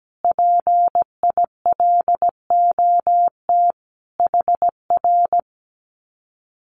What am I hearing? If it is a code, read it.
Morse code, heard as a keyed tone, about 17 wpm, 700 Hz. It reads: PILOT HR
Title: Common QSO Phrases Encore 17wpm